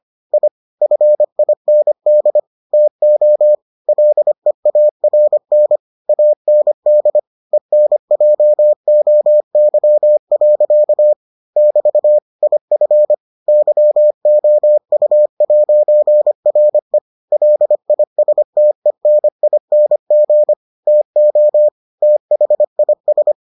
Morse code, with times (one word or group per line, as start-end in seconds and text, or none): 0.3-0.5 I
0.8-2.4 FIND
2.7-3.5 TO
3.9-5.8 LEARN
6.1-7.2 AND
7.5-11.1 ENJOY.
11.6-12.2 <BT>
12.4-13.1 IF
13.5-17.0 YOU'RE
17.3-20.5 LISTENING
20.9-21.7 TO
22.0-23.3 THIS